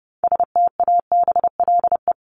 Morse code, heard as a keyed tone, about 30 wpm, 700 Hz. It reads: STABLE
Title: Top 1251-1500 Words 30wpm